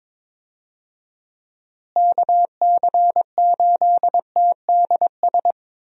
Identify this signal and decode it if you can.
Morse code, heard as a keyed tone, about 22 wpm, 700 Hz. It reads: KC8TDS
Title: USA Call Signs - 6 Characters - Repeated 3x - 22wpm